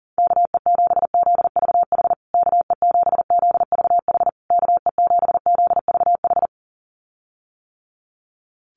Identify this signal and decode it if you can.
Morse code, heard as a keyed tone, about 40 wpm, 700 Hz. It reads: KE7ZVH KE7ZVH KE7ZVH
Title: USA Call Signs Repeated 3x 40wpm